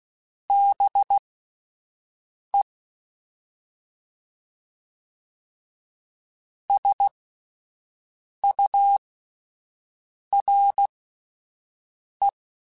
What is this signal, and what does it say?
Morse code, heard as a keyed tone, about 16 wpm, spaced as for 5 wpm, 800 Hz. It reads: BE SURE